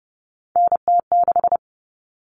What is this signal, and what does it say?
Morse code, heard as a keyed tone, about 30 wpm, 700 Hz. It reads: NT6
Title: All Cut Numbers - 3-Digit 30wpm